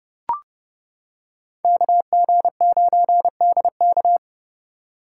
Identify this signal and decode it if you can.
Morse code, heard as a keyed tone, about 30 wpm, 700 Hz. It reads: KG9DK